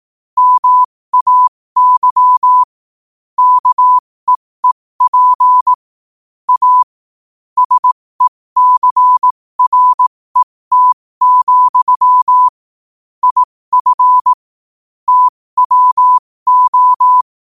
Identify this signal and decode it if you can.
Morse code, heard as a keyed tone, about 18 wpm, spaced as for 15 wpm, 1000 Hz. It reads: MAY KEEP A SECRET, IF TWO
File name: qotdfiles / 15wpm